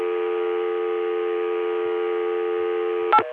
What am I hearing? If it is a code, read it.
Dial tone (North America), 350 + 440 Hz.